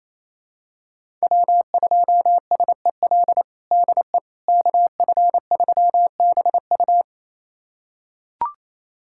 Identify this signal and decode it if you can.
Morse code, heard as a keyed tone, about 28 wpm, 700 Hz. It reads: W2SEL DE KF3BU
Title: USA Callsign DE Callsign - 5 Characters - 28wpm